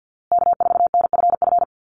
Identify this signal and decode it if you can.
Morse code, heard as a keyed tone, about 50 wpm, 700 Hz. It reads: K4NFF
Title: USA Call Signs Repeated 2x 50wpm